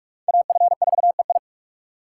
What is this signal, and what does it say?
Morse code, heard as a keyed tone, about 41 wpm, 700 Hz. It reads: AF4EI
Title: USA Call Signs - 5 Characters - 3x Character Spacing - 45wpm